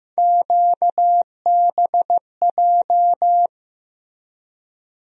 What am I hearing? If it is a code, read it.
Morse code, heard as a keyed tone, about 15 wpm, 700 Hz. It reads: QBJ
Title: USA Call Sign Postfixes - 3 Letters 15wpm